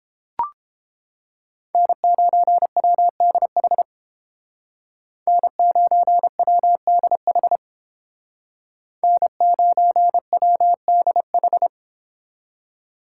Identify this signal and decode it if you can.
Morse code, heard as a keyed tone, about 30 wpm, 700 Hz. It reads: N9WDH N9WDH N9WDH